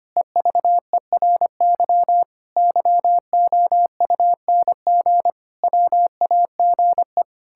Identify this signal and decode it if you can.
Morse code, heard as a keyed tone, about 25 wpm, 700 Hz. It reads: EVERY YOUNG WAGE
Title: Pangram Challenge v2 - 25wpm